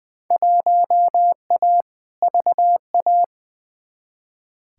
Morse code, heard as a keyed tone, about 20 wpm, 700 Hz. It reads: 1A VA
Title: Field Day Exchanges Encore - 20wpm